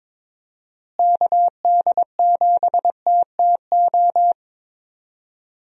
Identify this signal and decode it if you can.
Morse code, heard as a keyed tone, about 22 wpm, 700 Hz. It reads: KD7TTO
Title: USA Call Signs - 6 Characters - Repeated 3x - 22wpm